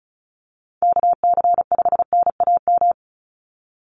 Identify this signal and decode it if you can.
Morse code, heard as a keyed tone, about 34 wpm, 700 Hz. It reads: KC5NAM
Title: USA Call Signs 35wpm